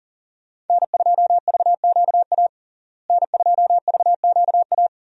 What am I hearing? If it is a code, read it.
Morse code, heard as a keyed tone, about 40 wpm, 700 Hz. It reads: N2VQA N2VQA
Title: USA Call Signs - 5 Characters - Repeated 2x - 40wpm